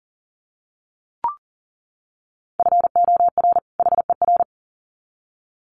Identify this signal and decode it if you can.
Morse code, heard as a keyed tone, about 40 wpm, 700 Hz. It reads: FOR HER
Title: Sets of 2 Words 40wpm